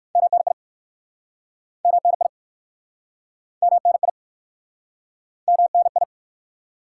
Morse code, heard as a keyed tone, about 59 wpm, 700 Hz. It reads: MNI MNI MNI MNI